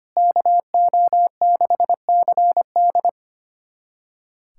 Morse code, heard as a keyed tone, about 25 wpm, 700 Hz. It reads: KO6CD
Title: USA Callsign DE Callsign 25wpm